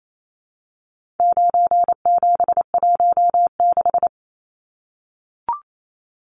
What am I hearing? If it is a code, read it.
Morse code, heard as a keyed tone, about 28 wpm, 700 Hz. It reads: 9716
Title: Numbers - 4-Digit 28wpm